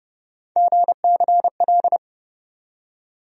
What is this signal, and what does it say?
Morse code, heard as a keyed tone, about 32 wpm, 700 Hz. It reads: GCL
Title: USA Call Sign Postfixes - 3 Letters 30wpm